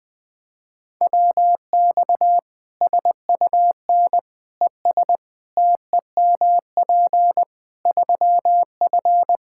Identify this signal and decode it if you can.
Morse code, heard as a keyed tone, about 20 wpm, 700 Hz. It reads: WX SUN ES TEMP 3F
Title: Practice QSO - Exchange 2 - Info - 20wpm